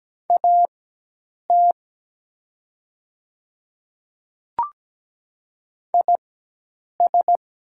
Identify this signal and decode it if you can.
Morse code, heard as a keyed tone, about 17 wpm, spaced as for 8 wpm, 700 Hz. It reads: AT IS
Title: 4x Character Spacing - Top 100 Words - 17wpm